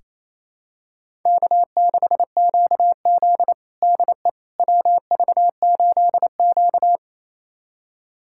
Morse code, heard as a keyed tone, about 28 wpm, 700 Hz. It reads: K6QZ DE WV8Q
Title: USA Callsign DE Callsign - 4 Characters - 28wpm